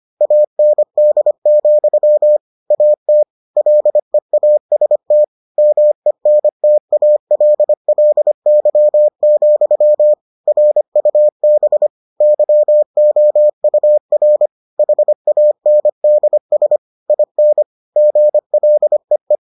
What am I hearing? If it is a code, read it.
Morse code, heard as a keyed tone, about 25 wpm, 600 Hz. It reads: AND, AT LEAST MENTALLY, RUB YOUR HANDS IN GLEE